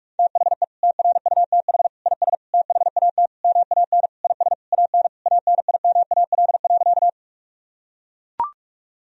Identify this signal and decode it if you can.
Morse code, heard as a keyed tone, about 41 wpm, 700 Hz. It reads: THE TRUTH IS THAT MAN IS AN ANIMAL.
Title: Sentences from Top 1000 Words - Review Part 1 - 45wpm